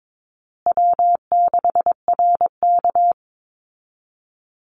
Morse code, heard as a keyed tone, about 22 wpm, 700 Hz. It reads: W6RK
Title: Easy Call Signs Repeated 3x - 22wpm